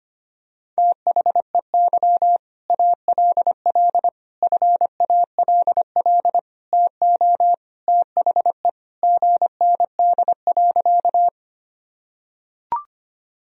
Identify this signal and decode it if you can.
Morse code, heard as a keyed tone, about 25 wpm, 700 Hz. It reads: THEY ALL FALL TO THE GND.
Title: Sentences from Top 500 Words and Abbreviations 25wpm